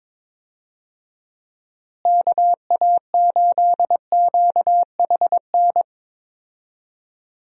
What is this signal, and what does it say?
Morse code, heard as a keyed tone, about 22 wpm, 700 Hz. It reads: KA8QHN
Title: USA Call Signs - 6 Characters - Repeated 3x - 22wpm